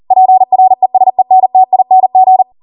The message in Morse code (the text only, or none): PRESENTING